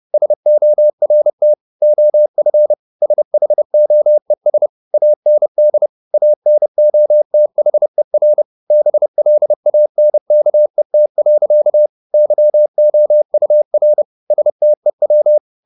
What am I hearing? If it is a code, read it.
Morse code, heard as a keyed tone, about 30 wpm, 600 Hz. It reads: SORT OF SHOES AND ANOTHER BLANKET. YOUR STEW
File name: the old man and the sea / the old man and the sea 1 30wpm